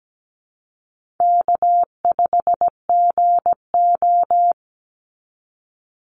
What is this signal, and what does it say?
Morse code, heard as a keyed tone, about 17 wpm, 700 Hz. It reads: K5GO
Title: Call Signs Repeated 2x - 17wpm